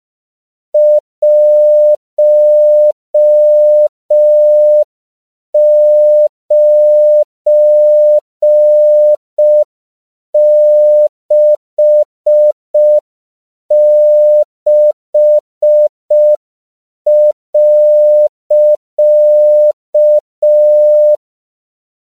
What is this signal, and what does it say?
Morse code, heard as a keyed tone, about 5 wpm, 600 Hz. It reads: 1966.
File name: CW / 5PPM